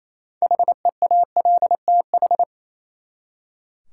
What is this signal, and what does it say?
Morse code, heard as a keyed tone, about 28 wpm, 700 Hz. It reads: HEALTH
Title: Practice QSO - Exchange 3 - More Info - 28wpm